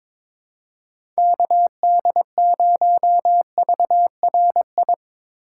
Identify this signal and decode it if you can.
Morse code, heard as a keyed tone, about 22 wpm, 700 Hz. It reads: KD0VRI
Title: USA Call Signs - 6 Characters - 2x Character Spacing - 22wpm